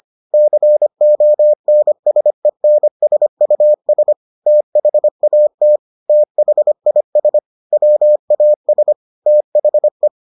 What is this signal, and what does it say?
Morse code, heard as a keyed tone, about 25 wpm, 600 Hz. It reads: CONSENSUS THAT THIS WAS THE